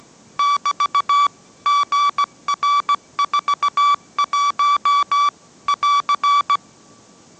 Morse code, heard as a keyed tone, about 17 wpm, 1150 Hz. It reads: <BT> GR41 <AR>
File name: mrx / mrx080 18wpm